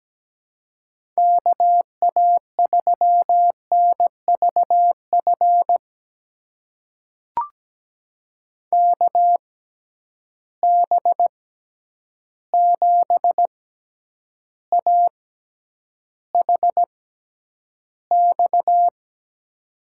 Morse code, heard as a keyed tone, about 17 wpm, spaced as for 9 wpm, 700 Hz. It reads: KA3NVF KB7AHX
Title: USA Call Signs - 6 Characters - 6x Character Spacing - 17wpm